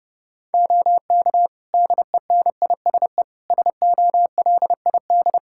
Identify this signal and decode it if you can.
Morse code, heard as a keyed tone, about 30 wpm, 700 Hz. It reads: OK DENISE SOLID
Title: Practice QSO - Exchange 3 - More Info - 30wpm